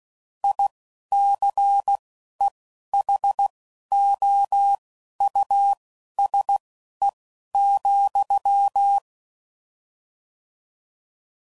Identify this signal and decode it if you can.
Morse code, heard as a keyed tone, about 16 wpm, spaced as for 11 wpm, 800 Hz. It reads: ICEHOUSE,
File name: Suffixes / General / Houses